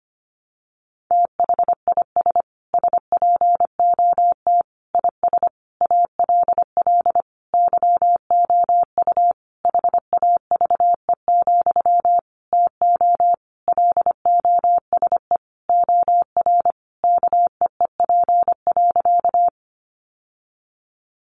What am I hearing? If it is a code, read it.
Morse code, heard as a keyed tone, about 25 wpm, 700 Hz. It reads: THIS SPOT IS ALL YOU HAVE, TO LOSE OR KEEP.
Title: Sentences from Top 1100 Words 25wpm